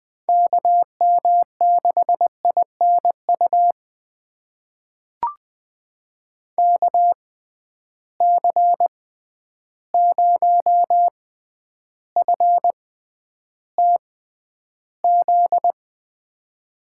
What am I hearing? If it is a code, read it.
Morse code, heard as a keyed tone, about 20 wpm, spaced as for 10 wpm, 700 Hz. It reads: KM6INU KC0FTZ
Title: USA Call Signs - 6 Characters - 6x Character Spacing - 20wpm